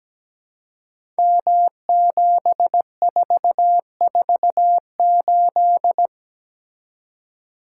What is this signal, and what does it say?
Morse code, heard as a keyed tone, about 17 wpm, 700 Hz. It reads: M7448
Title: Code Groups - 5 Letter:Numbers - 17wpm